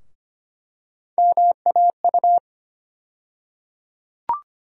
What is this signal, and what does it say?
Morse code, heard as a keyed tone, about 25 wpm, 700 Hz. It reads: MAU
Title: USA Call Sign Postfixes - 3 Letters 25wpm